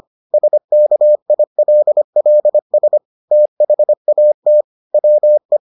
Morse code, heard as a keyed tone, about 25 wpm, 600 Hz. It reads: SKILLS THAT WE